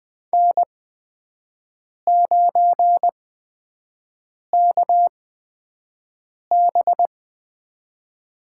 Morse code, heard as a keyed tone, about 20 wpm, spaced as for 5 wpm, 700 Hz. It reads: N9KB
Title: USA Call Signs - 5 Characters - 8x Character Spacing - 20wpm